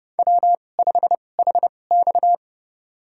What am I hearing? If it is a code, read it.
Morse code, heard as a keyed tone, about 30 wpm, spaced as for 22 wpm, 700 Hz. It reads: W5HX
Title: USA Call Signs - 4 Characters - 2x Character Spacing - 30wpm